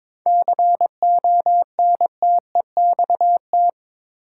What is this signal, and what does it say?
Morse code, heard as a keyed tone, about 21 wpm, 700 Hz. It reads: CONTEXT
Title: Sets of 4 Words - New Word Drill - 22wpm